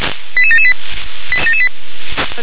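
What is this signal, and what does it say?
Five-tone selective call (ZVEI), heard as quick runs of five tones, tones of about 70 ms. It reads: 88778
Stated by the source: ZVEI five-tone selective call 88951 received by SDR.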